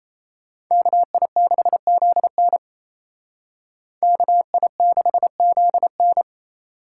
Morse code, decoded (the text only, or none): KI6ZN KI6ZN